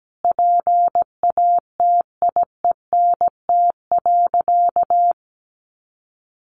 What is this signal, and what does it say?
Morse code, heard as a keyed tone, about 17 wpm, 700 Hz. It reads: PATIENT.
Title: Easy Phrases 17wpm